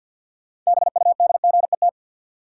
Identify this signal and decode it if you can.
Morse code, heard as a keyed tone, about 53 wpm, 700 Hz. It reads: BUDGET
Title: Sentences from Top 1200 Words - New Word Drill Alt - 50wpm